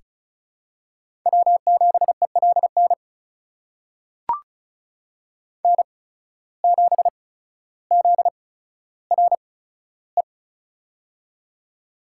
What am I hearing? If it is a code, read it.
Morse code, heard as a keyed tone, about 36 wpm, spaced as for 15 wpm, 700 Hz. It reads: W7ELN N7ZRE
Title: USA Call Signs - 5 Characters - 8x Character Spacing - 35wpm